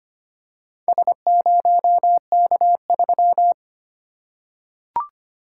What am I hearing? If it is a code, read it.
Morse code, heard as a keyed tone, about 25 wpm, 700 Hz. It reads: S0K3
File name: practice / 25wpm / Four Letter-Number Combinations - 25wpm